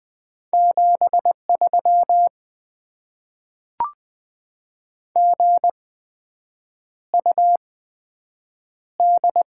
Morse code, heard as a keyed tone, about 20 wpm, spaced as for 7 wpm, 700 Hz. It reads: 73 GUD
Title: Top 20 QSO Elements - 8x Character Spacing - 20wpm